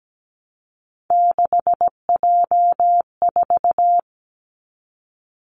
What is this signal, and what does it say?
Morse code, heard as a keyed tone, about 17 wpm, 700 Hz. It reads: 6J4
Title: Three Challenging Letter-Number Combinations 17wpm